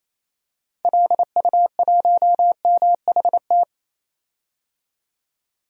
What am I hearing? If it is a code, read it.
Morse code, heard as a keyed tone, about 28 wpm, 700 Hz. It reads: LU1MHT
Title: DXCC-100 Call Signs - 6 Characters - 28wpm